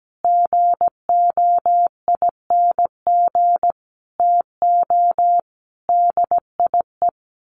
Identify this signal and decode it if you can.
Morse code, heard as a keyed tone, about 17 wpm, 700 Hz. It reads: GOING TO DIE
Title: A Sound Of Thunder 17wpm